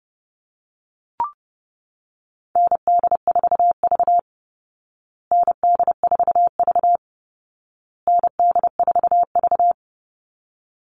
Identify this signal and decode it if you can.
Morse code, heard as a keyed tone, about 30 wpm, 700 Hz. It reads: ND4V ND4V ND4V
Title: Call Signs Repeated 2x - 30wpm